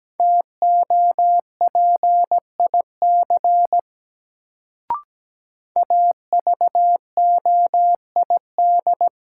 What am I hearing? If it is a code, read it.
Morse code, heard as a keyed tone, about 17 wpm, 700 Hz. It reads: TOPIC AVOID
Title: Sentences from Top 1200 Words - New Word Drill Alt - 17wpm